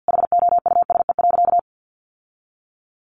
Morse code, heard as a keyed tone, about 50 wpm, 700 Hz. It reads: HOUSE.
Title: Sentences from Top 600 Words 50wpm